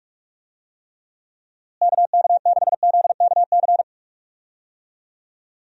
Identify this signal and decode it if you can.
Morse code, heard as a keyed tone, about 46 wpm, 700 Hz. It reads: KK6ZKC
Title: USA Call Signs - 6 Characters - Repeated 3x - 45wpm